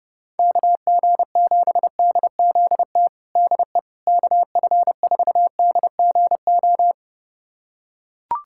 Morse code, heard as a keyed tone, about 30 wpm, 700 Hz. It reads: KG7DZT DE KF4DGO
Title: USA Callsign DE Callsign - 6 Characters - 30wpm